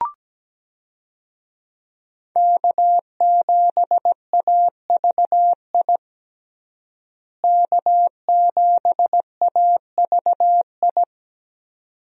Morse code, heard as a keyed tone, about 17 wpm, 700 Hz. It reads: K7AVI K7AVI
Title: USA Call Signs - 5 Characters - Repeated 2x - 17wpm